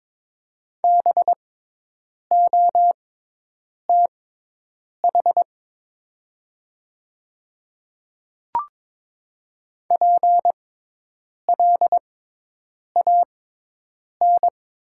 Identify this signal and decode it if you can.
Morse code, heard as a keyed tone, about 22 wpm, spaced as for 8 wpm, 700 Hz. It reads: BOTH PLAN
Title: 6x Character Spacing - Top 500 Words - 22wpm